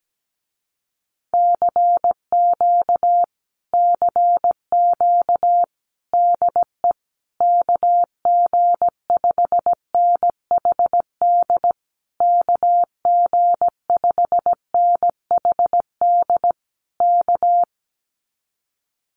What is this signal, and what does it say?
Morse code, heard as a keyed tone, about 17 wpm, 700 Hz. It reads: CQ CQ DE KG5NHD KG5NHD K
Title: Common QSO Phrases Encore 17wpm